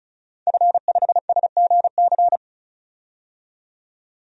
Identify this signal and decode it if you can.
Morse code, heard as a keyed tone, about 34 wpm, 700 Hz. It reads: F5SGC
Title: DXCC-100 Call Signs - 5 Characters - 35wpm